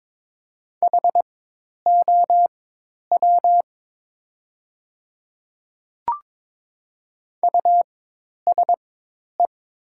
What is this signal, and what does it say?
Morse code, heard as a keyed tone, about 22 wpm, spaced as for 10 wpm, 700 Hz. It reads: HOW USE
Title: 4x Character Spacing - Top 100 Words - 22wpm